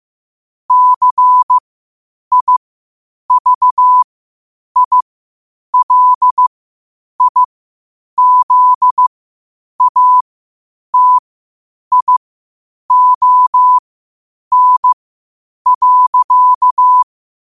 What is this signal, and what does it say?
Morse code, heard as a keyed tone, about 15 wpm, spaced as for 8 wpm, 1000 Hz. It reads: CIVILIZATION.
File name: qotdfiles / 8wpm